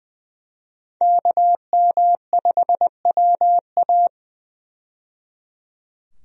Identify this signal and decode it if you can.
Morse code, heard as a keyed tone, about 20 wpm, 700 Hz. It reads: KM5WA
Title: USA Call Signs - 5 Characters - 20wpm